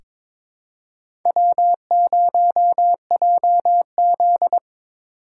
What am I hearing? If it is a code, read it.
Morse code, heard as a keyed tone, about 21 wpm, 700 Hz. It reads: W0JZ
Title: USA Call Signs - 4 Characters - 6x Character Spacing - 22wpm